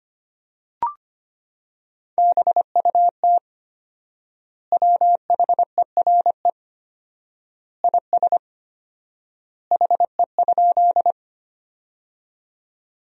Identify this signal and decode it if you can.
Morse code, heard as a keyed tone, about 25 wpm, 700 Hz. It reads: BUT WHERE IS HE?
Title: 4x Word-Spacing - Sentences from Top 200 Words - 25wpm